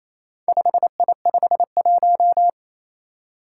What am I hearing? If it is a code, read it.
Morse code, heard as a keyed tone, about 27 wpm, 700 Hz. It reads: 5I51
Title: Four Challenging Letter-Number Combinations 28wpm